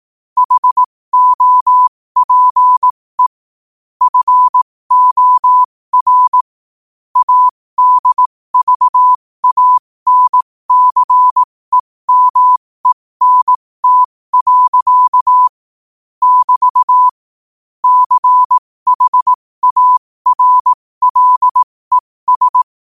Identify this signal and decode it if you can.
Morse code, heard as a keyed tone, about 18 wpm, spaced as for 15 wpm, 1000 Hz. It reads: HOPE FOR ADVANCEMENT. <BT> CHARLES
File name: qotdfiles / 15wpm